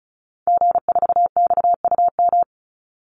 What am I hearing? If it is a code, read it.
Morse code, heard as a keyed tone, about 34 wpm, 700 Hz. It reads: G4XUM
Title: Call Signs - 35wpm